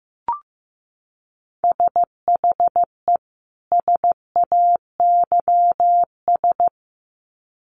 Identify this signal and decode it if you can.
Morse code, heard as a keyed tone, about 15 wpm, 700 Hz. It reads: SHE SAYS
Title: Sets of 2 Words 15wpm